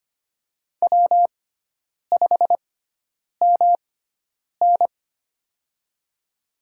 Morse code, heard as a keyed tone, about 25 wpm, spaced as for 9 wpm, 700 Hz. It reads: W5MN
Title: USA Call Signs - 4 Characters - 6x Character Spacing - 25wpm